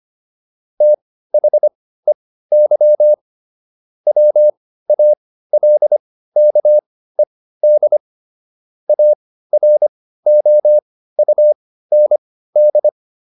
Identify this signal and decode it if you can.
Morse code, heard as a keyed tone, about 25 wpm, spaced as for 15 wpm, 600 Hz. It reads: THEY WALKED AROUND
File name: SS119 15